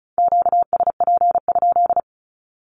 Morse code, heard as a keyed tone, about 34 wpm, 700 Hz. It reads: QSP?
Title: Q-Codes Repeated 3x 35wpm